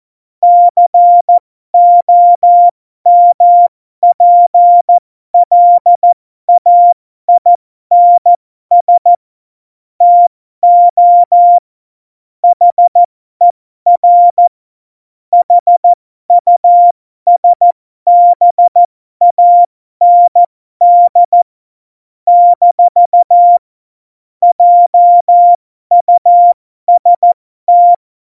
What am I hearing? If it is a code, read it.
Morse code, heard as a keyed tone, about 14 wpm, spaced as for 12 wpm, 700 Hz. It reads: COMPLAINS TO HER HUSBAND - JUST
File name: Morsekurs / Witze